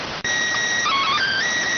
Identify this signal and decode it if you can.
SSTV (slow-scan television), PD 290.